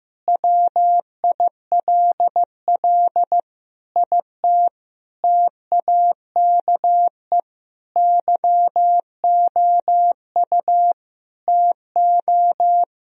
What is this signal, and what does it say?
Morse code, heard as a keyed tone, about 15 wpm, 700 Hz. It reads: WILL IT TAKE YOU TO
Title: Sentences from Top 1000 Words - Review Part 2 - 15wpm